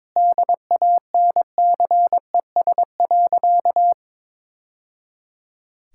Morse code, heard as a keyed tone, about 22 wpm, 700 Hz. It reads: DANCES.
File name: practice / 22wpm / Sentences with 3-Word Phrases - 22wpm